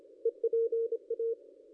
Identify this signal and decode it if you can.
Morse code, heard as a keyed tone, about 26 wpm, 460 Hz. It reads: EPA
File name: contests / ss-cw-k5tr-op-n5rz -0600